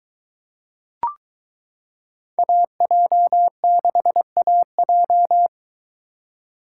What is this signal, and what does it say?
Morse code, heard as a keyed tone, about 23 wpm, 700 Hz. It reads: AJ6AJ